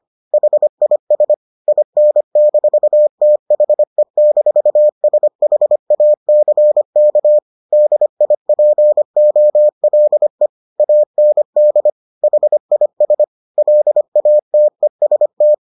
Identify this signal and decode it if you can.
Morse code, heard as a keyed tone, about 25 wpm, 600 Hz. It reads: HIS IN-THE-SHACK DIPOLE AND HIS LATEST